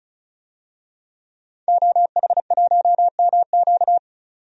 Morse code, heard as a keyed tone, about 36 wpm, 700 Hz. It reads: OH1MQ
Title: DXCC-100 Call Signs - 5 Characters - 35wpm